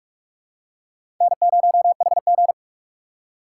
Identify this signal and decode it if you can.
Morse code, heard as a keyed tone, about 44 wpm, 690 Hz. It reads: N0HG
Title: USA Call Signs - 4 Characters - 3x Character Spacing - 45wpm